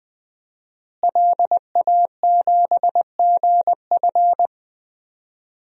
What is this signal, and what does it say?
Morse code, heard as a keyed tone, about 20 wpm, 700 Hz. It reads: LA7GF
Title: DXCC-100 Call Signs - 5 Characters - Repeated 3x - 20wpm